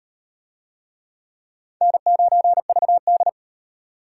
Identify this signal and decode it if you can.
Morse code, heard as a keyed tone, about 38 wpm, 700 Hz. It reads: N9VD